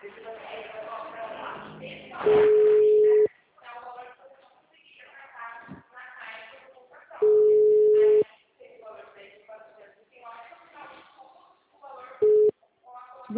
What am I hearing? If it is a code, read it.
Ringback tone (Europe, 425 Hz), with speech.
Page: Sumare - SP